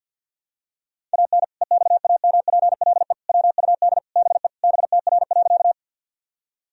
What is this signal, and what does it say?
Morse code, heard as a keyed tone, about 47 wpm, 700 Hz. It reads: AN EXAMPLE WUD BE BTR.
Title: Sentences from Top 400 Words and Abbreviations 50wpm